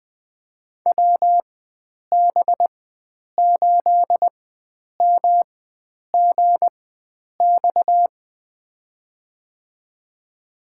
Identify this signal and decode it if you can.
Morse code, heard as a keyed tone, about 20 wpm, spaced as for 9 wpm, 700 Hz. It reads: WB8MGX
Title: USA Call Signs - 4x Character Spacing - 20wpm